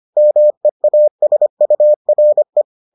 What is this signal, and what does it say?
Morse code, heard as a keyed tone, about 25 wpm, 600 Hz. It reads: MEASURE